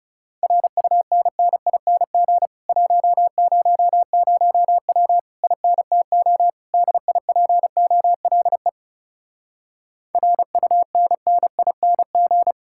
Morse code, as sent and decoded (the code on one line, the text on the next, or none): .-. ..- -. -. .. -. --. / .---- ----- ----- .-- / .. -. - --- / -.. .. .--. --- .-.. / .-. ..- -. -. .. -. --.
RUNNING 100W INTO DIPOLE RUNNING